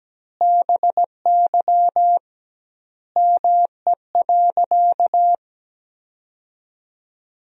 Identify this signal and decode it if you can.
Morse code, heard as a keyed tone, about 17 wpm, 700 Hz. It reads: BY ME.
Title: 2x Word-Spacing - Sentences from Top 500 Words - 17wpm